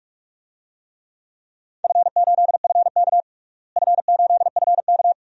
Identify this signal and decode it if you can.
Morse code, heard as a keyed tone, about 42 wpm, 700 Hz. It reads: F8FK F8FK
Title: DXCC-100 Call Signs - 4 Characters - Repeated 2x - 45wpm